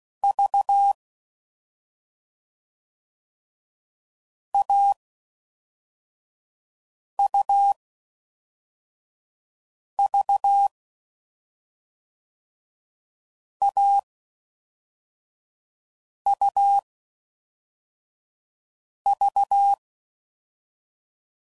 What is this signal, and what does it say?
Morse code, heard as a keyed tone, about 16 wpm, 800 Hz. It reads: V A U V A U V
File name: Combinations / Difficult Combos / AUV Short